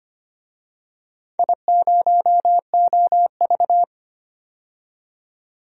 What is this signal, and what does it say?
Morse code, heard as a keyed tone, about 25 wpm, 700 Hz. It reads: I0OV
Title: DXCC-100 Call Signs - 4 Characters - 25wpm